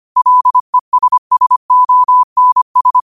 Morse code, heard as a keyed tone, about 25 wpm, 1000 Hz. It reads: LESSONS